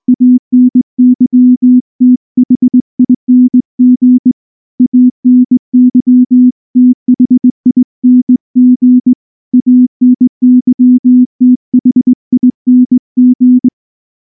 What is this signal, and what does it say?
Morse code, heard as a keyed tone, about 19 wpm, 250 Hz. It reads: ANYTHING ANYTHING ANYTHING
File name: morse / morse-training-20WPM-9